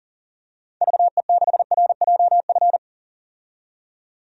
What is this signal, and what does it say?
Morse code, heard as a keyed tone, about 40 wpm, 700 Hz. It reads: VE6RJF
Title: DXCC-100 Call Signs - 6 Characters - 40wpm